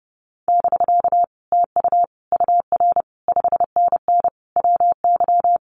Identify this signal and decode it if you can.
Morse code, heard as a keyed tone, about 30 wpm, 700 Hz. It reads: <BK> TU UR 5NN WY
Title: Common QSO Phrases Encore 30wpm